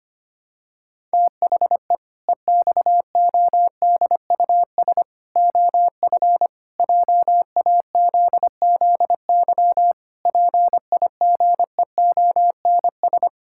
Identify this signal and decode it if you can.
Morse code, heard as a keyed tone, about 25 wpm, 700 Hz. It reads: THE EXODUS OF JAZZY PIGEONS
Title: Pangram Challenge v2 - 25wpm